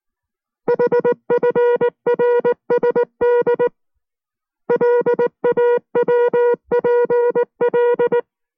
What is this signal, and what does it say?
Morse code, heard as a keyed tone, about 19 wpm, 480 Hz. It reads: HFRSD LAWPL